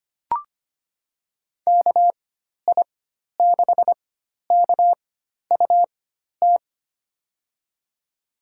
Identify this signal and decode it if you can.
Morse code, heard as a keyed tone, about 25 wpm, spaced as for 12 wpm, 700 Hz. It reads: KI6KUT
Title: USA Call Signs - 6 Characters - 4x Character Spacing - 25wpm